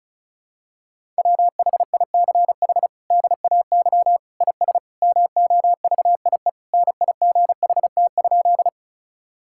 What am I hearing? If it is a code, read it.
Morse code, heard as a keyed tone, about 36 wpm, 700 Hz. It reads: WHICH DAY IS MOVIE NIGHT?
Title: Sentences from Top 1000 Words - Review Part 5 - 35wpm